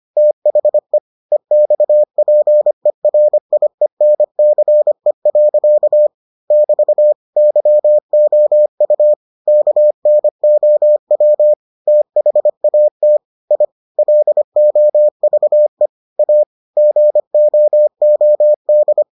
Morse code, heard as a keyed tone, about 25 wpm, 600 Hz. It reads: THE EXPERIENCE. <BT> YOU KNOW THAT I LOVE A GOOD